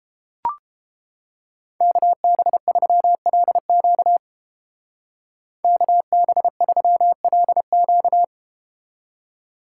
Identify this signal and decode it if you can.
Morse code, heard as a keyed tone, about 32 wpm, 700 Hz. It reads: KB3LQ KB3LQ